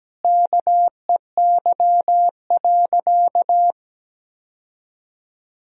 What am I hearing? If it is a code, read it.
Morse code, heard as a keyed tone, about 17 wpm, 700 Hz. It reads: KEY.
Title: Sentences from Top 1000 Words - Review Part 4 - 17wpm